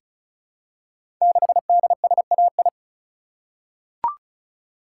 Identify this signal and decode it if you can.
Morse code, heard as a keyed tone, about 36 wpm, 700 Hz. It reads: 6DSAI
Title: Code Groups - 5 Letter:Numbers - 35wpm